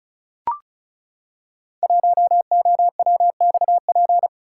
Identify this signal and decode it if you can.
Morse code, heard as a keyed tone, about 36 wpm, 700 Hz. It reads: 1OWXP
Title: Code Groups - 5 Letter:Numbers - 35wpm